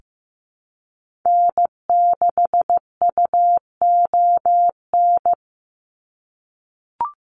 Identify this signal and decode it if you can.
Morse code, heard as a keyed tone, about 15 wpm, 700 Hz. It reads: N6UON
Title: USA Call Signs 15wpm